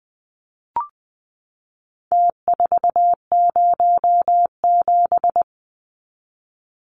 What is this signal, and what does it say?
Morse code, heard as a keyed tone, about 20 wpm, 700 Hz. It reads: T407
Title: Numbers - Contest Style - 4-Digit 20wpm